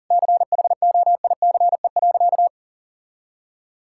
Morse code, heard as a keyed tone, about 37 wpm, 700 Hz. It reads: CHOICE.
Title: Sentences from Top 1000 Words - Review Part 6 - 40wpm